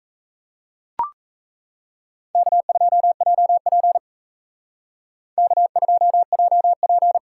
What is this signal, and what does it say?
Morse code, heard as a keyed tone, about 34 wpm, 700 Hz. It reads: K2JP K2JP
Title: Speed Racing - USA 1x2 Call Signs 28wpm